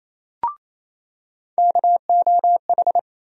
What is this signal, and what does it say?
Morse code, heard as a keyed tone, about 28 wpm, 700 Hz. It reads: KOH
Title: USA Call Sign Postfixes - 3 Letters 28wpm